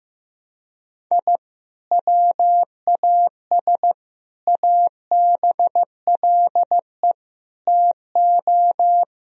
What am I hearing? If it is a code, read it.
Morse code, heard as a keyed tone, about 15 wpm, 700 Hz. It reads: I WAS ABLE TO
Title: Sentences from Top 1000 Words - Review Part 2 - 15wpm